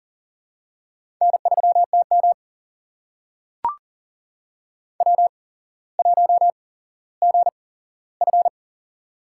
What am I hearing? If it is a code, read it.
Morse code, heard as a keyed tone, about 40 wpm, spaced as for 17 wpm, 700 Hz. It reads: N3TM W1GF
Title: USA Call Signs - 4 Characters - 8x Character Spacing - 40wpm